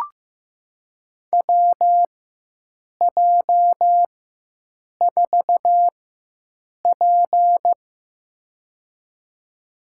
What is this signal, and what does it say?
Morse code, heard as a keyed tone, about 15 wpm, spaced as for 7 wpm, 700 Hz. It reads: WJ4P
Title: USA Call Signs - 4 Characters - 4x Character Spacing - 15wpm